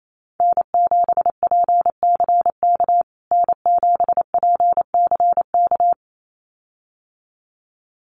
Morse code, heard as a keyed tone, about 28 wpm, 700 Hz. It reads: N7PCK N7PCK
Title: USA Call Signs Repeated 2x 28wpm